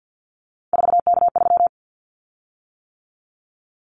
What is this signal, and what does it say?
Morse code, heard as a keyed tone, about 50 wpm, 700 Hz. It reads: 4K3
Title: Callsign Prefixes for Top 100 DXCC - 50wpm